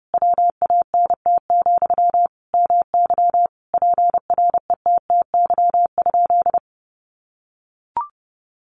Morse code, heard as a keyed tone, about 30 wpm, 700 Hz. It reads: WANT, MY PRETTY?
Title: Sentences from Top 1000 Words 30wpm